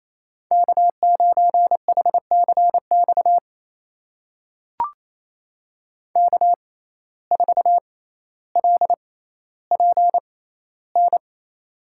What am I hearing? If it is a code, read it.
Morse code, heard as a keyed tone, about 28 wpm, spaced as for 14 wpm, 700 Hz. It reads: K9HCX K4LPN
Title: USA Call Signs - 5 Characters - 6x Character Spacing - 28wpm